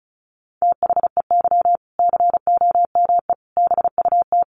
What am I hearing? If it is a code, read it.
Morse code, heard as a keyed tone, about 34 wpm, 700 Hz. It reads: THEY COME BUT
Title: Sentences from Top 1400 Words 35wpm